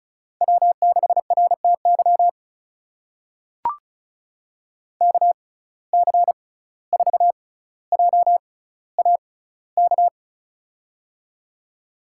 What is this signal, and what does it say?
Morse code, heard as a keyed tone, about 36 wpm, spaced as for 17 wpm, 700 Hz. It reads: W6RTY KC4JAK
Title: USA Call Signs - 6x Character Spacing - 35wpm